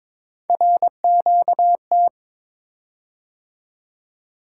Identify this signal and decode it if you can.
Morse code, heard as a keyed tone, about 22 wpm, 700 Hz. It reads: RQT
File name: practice / 22wpm / Three Letter-Number Combinations - 22wpm